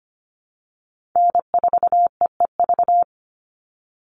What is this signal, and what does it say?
Morse code, heard as a keyed tone, about 25 wpm, 700 Hz. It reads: N4EEV
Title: Call Signs - 25wpm